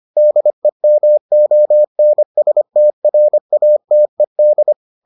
Morse code, heard as a keyed tone, about 25 wpm, 600 Hz. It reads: DEMONSTRATED